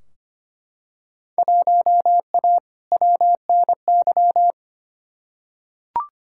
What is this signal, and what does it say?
Morse code, heard as a keyed tone, about 25 wpm, 700 Hz. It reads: 1A WNY
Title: Field Day Exchanges Encore - 25wpm